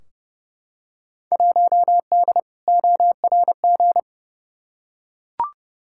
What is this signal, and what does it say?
Morse code, heard as a keyed tone, about 30 wpm, 700 Hz. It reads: 1D ORG
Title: Field Day Exchanges - 30wpm